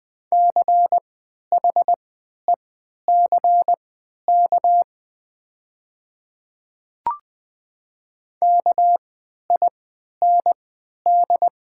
Morse code, heard as a keyed tone, about 20 wpm, spaced as for 11 wpm, 700 Hz. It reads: CHECK KIND
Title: 3x Character Spacing - Top 500 Words - 20wpm